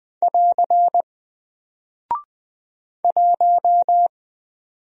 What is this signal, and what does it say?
Morse code, heard as a keyed tone, about 20 wpm, 700 Hz. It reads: <AR> 1
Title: Single Character (Letter:Number:Punctuation:Prosign) Alt 20wpm